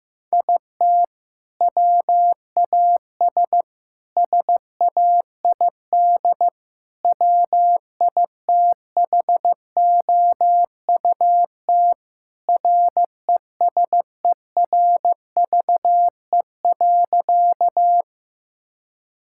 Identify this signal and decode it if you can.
Morse code, heard as a keyed tone, about 15 wpm, 700 Hz. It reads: IT WAS SAID WITHOUT RESERVE.
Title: Sentences from Top 1000 Words - Review Part 6 - 15wpm